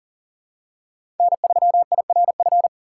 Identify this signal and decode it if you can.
Morse code, heard as a keyed tone, about 40 wpm, 700 Hz. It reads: N3IRF
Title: USA Call Signs - 5 Characters - 4x Character Spacing - 40wpm